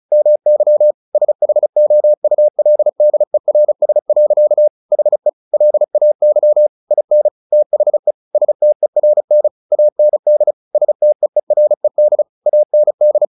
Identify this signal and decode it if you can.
Morse code, heard as a keyed tone, about 34 wpm, 600 Hz. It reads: MY SHOULDERS. HE LAY IN THE STERN AND STEERED AND
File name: the old man and the sea 2 35wpm